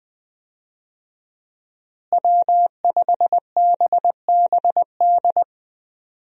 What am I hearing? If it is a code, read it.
Morse code, heard as a keyed tone, about 21 wpm, 700 Hz. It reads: W5BBD
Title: USA Call Signs - 5 Characters - Repeated 2x - 20wpm